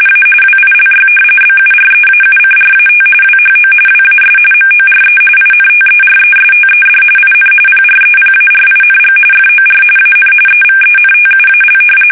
RTTY, about 45.45 baud. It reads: A WILY BUREAUCRAT WHO WAS MORE INTERESTED IN PREPARING FOR AN EVENTUAL
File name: fsk850